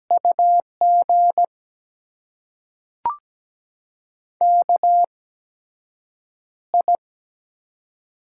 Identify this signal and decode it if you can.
Morse code, heard as a keyed tone, about 17 wpm, spaced as for 7 wpm, 700 Hz. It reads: UG KI
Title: USA Call Signs - 4 Characters - 8x Character Spacing - 17wpm